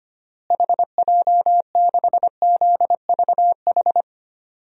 Morse code, heard as a keyed tone, about 25 wpm, 700 Hz. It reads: HJ6ZVH
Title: DXCC-100 Call Signs - 6 Characters - Repeated 3x - 25wpm